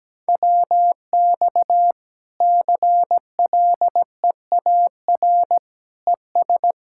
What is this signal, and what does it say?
Morse code, heard as a keyed tone, about 17 wpm, 700 Hz. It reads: WX CLEAR ES
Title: K1USN SST QSO Practice 17wpm